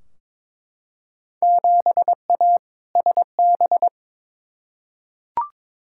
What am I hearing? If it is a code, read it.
Morse code, heard as a keyed tone, about 21 wpm, 700 Hz. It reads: 7A SB
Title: Field Day Exchanges Encore - 22wpm